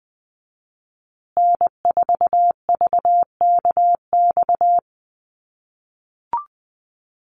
Morse code, heard as a keyed tone, about 20 wpm, 700 Hz. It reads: N4VKX
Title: USA Call Signs Repeated 3x 20wpm